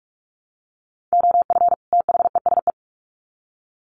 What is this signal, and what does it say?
Morse code, heard as a keyed tone, about 44 wpm, 700 Hz. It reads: OF THESE
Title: Sets of 2 Words 45wpm